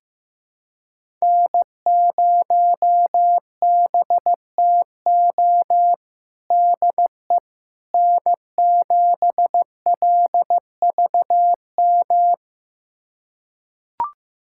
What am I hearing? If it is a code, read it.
Morse code, heard as a keyed tone, about 16 wpm, 700 Hz. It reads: N0BTO DE N7LVM
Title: USA Callsign DE Callsign - 5 Characters - 15wpm